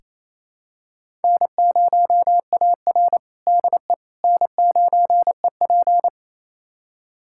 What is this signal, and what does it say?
Morse code, heard as a keyed tone, about 27 wpm, 700 Hz. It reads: N0AR DE N9EP
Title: USA Callsign DE Callsign - 4 Characters - 28wpm